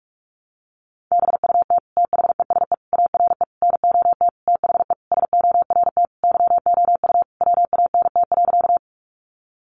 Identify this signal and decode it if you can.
Morse code, heard as a keyed tone, about 44 wpm, 700 Hz. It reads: BUT THESE ARE NOT THE SORT YOU WANT.
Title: Sentences from Top 1300 Words 45wpm